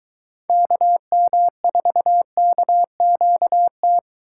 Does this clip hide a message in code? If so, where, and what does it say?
Yes: Morse code from 0.5 to 4.0 s, heard as a keyed tone, about 23 wpm, 700 Hz. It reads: KM4KQT